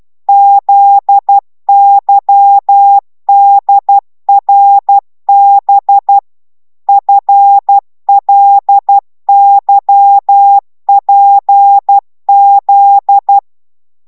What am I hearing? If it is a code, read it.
Morse code, heard as a keyed tone, about 12 wpm, 800 Hz. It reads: ZYDRB FLYPZ
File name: taviro tanulas / cw 69 19 10